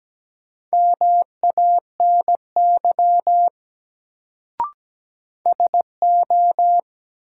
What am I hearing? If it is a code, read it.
Morse code, heard as a keyed tone, about 17 wpm, 700 Hz. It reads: MANY SO
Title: Sentences from Top 100 Words - New Word Drill - 17wpm